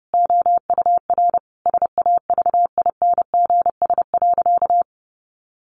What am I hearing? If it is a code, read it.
Morse code, heard as a keyed tone, about 30 wpm, 700 Hz. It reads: OUR SAVINGS.
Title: Sentences from Top 1400 Words 30wpm